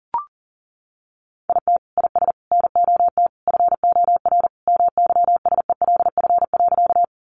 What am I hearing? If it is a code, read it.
Morse code, heard as a keyed tone, about 40 wpm, 700 Hz. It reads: IT IS NOT FOR MYSELF.
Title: Sentences from Top 1300 Words 40wpm